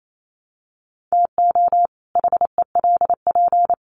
Morse code, heard as a keyed tone, about 28 wpm, 700 Hz. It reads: TO HELP
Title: Sets of 2 Words 28wpm